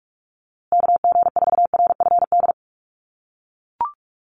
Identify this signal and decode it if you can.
Morse code, heard as a keyed tone, about 44 wpm, 700 Hz. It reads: KG4RFD
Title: USA Call Signs Repeated 3x 45wpm